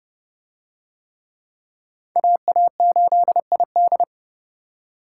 Morse code, heard as a keyed tone, about 29 wpm, 700 Hz. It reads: AA8ID